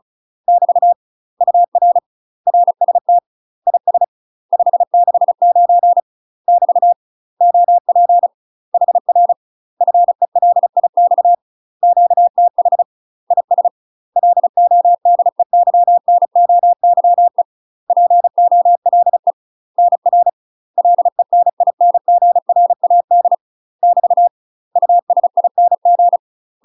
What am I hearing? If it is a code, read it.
Morse code, heard as a keyed tone, about 36 wpm, 700 Hz. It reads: <BT> UR RST IS 569 <BT> OP HR FELIX QTH IS LODEYNOYE POLE NR LENINGRAD <BT> USING
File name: UK MASTER QSOs TRAINING / 35WPM / UK MASTER QSOs 0010